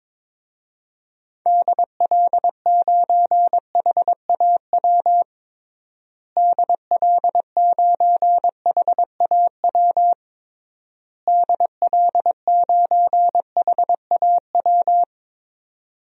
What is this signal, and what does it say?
Morse code, heard as a keyed tone, about 22 wpm, 700 Hz. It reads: DL9HAW DL9HAW DL9HAW
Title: DXCC-100 Call Signs - 6 Characters - Repeated 3x - 22wpm